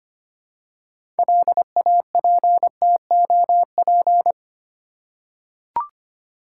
Morse code, heard as a keyed tone, about 25 wpm, 700 Hz. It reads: LAPTOP
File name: practice / 25wpm / Easy Words with Hints - 25wpm